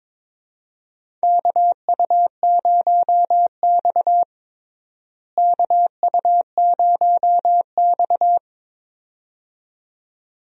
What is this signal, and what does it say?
Morse code, heard as a keyed tone, about 22 wpm, 700 Hz. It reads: KU0X KU0X
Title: USA Call Signs - 4 Characters - Repeated 2x - 22wpm